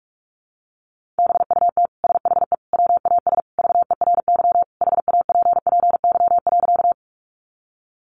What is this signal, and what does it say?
Morse code, heard as a keyed tone, about 46 wpm, 700 Hz. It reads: BUT SHE WAS VERY HAPPY.
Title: Sentences from Top 1000 Words 45wpm